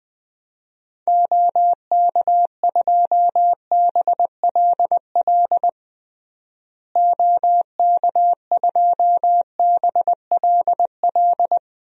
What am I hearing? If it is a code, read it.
Morse code, heard as a keyed tone, about 20 wpm, 700 Hz. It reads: OK2BLL OK2BLL
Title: DXCC-100 Call Signs - 6 Characters - Repeated 3x - 20wpm